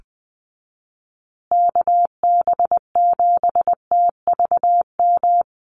Morse code, heard as a keyed tone, about 20 wpm, 700 Hz. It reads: KB7TVM
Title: USA Call Signs Repeated 3x 20wpm